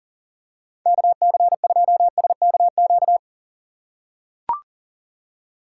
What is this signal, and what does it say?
Morse code, heard as a keyed tone, about 40 wpm, 700 Hz. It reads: KC2SKQ
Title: USA Call Signs - 6 Characters - 4x Character Spacing - 40wpm